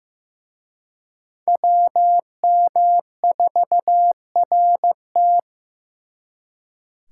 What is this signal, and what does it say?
Morse code, heard as a keyed tone, about 18 wpm, 700 Hz. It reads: WM4RT
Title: Speed Racing - USA 2x2 Call Signs 15wpm